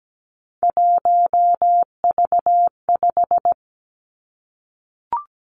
Morse code, heard as a keyed tone, about 17 wpm, 700 Hz. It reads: 1V5
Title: Three Challenging Letter-Number Combinations 17wpm